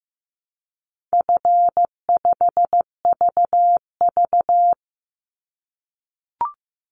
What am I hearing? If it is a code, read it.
Morse code, heard as a keyed tone, about 15 wpm, 700 Hz. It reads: F5VV
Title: Easy Call Signs - 15wpm